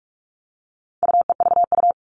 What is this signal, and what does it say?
Morse code, heard as a keyed tone, about 44 wpm, 700 Hz. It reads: UEVU
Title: All Cut Numbers - 4-Digit 45wpm